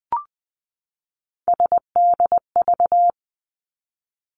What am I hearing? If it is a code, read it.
Morse code, heard as a keyed tone, about 20 wpm, 700 Hz. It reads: SDV
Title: Three Challenging Letter-Number Combinations 20wpm